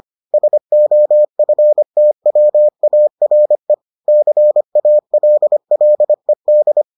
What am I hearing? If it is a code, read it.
Morse code, heard as a keyed tone, about 25 wpm, 600 Hz. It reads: SOFTWARE CALLED